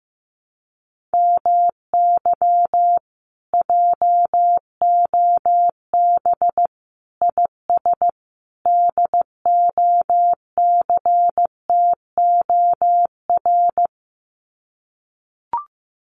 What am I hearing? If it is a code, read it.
Morse code, heard as a keyed tone, about 15 wpm, 700 Hz. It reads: MY JOB IS DOCTOR
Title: Common QSO Phrases Encore 15wpm